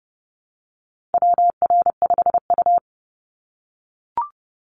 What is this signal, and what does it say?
Morse code, heard as a keyed tone, about 30 wpm, 700 Hz. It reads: WR5U
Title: Call Signs - 30wpm